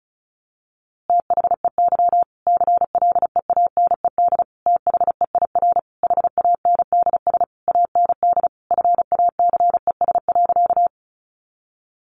Morse code, heard as a keyed tone, about 34 wpm, 700 Hz. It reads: THEY CLEANED THEIR HANDS AND FACES.
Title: A Sound Of Thunder 35wpm